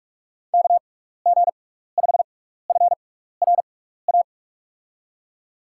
Morse code, heard as a keyed tone, about 42 wpm, 700 Hz. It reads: K G 5 F R A
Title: USA Call Signs - 6 Characters - 6x Character Spacing - 45wpm